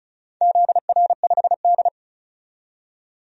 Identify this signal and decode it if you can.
Morse code, heard as a keyed tone, about 34 wpm, 700 Hz. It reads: ZR5D
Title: DXCC-100 Call Signs - 4 Characters - 35wpm